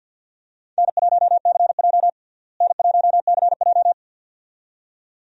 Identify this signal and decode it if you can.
Morse code, heard as a keyed tone, about 53 wpm, 700 Hz. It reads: N1CJ N1CJ
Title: USA Call Signs - 4 Characters - Repeated 2x - 50wpm